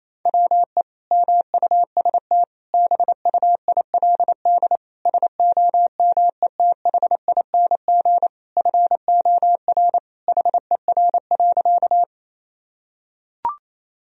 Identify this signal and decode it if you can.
Morse code, heard as a keyed tone, about 28 wpm, 700 Hz. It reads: WE MUST BUILD SOMETHING FOR HER.
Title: Sentences from Top 1000 Words - Review Part 1 - 28wpm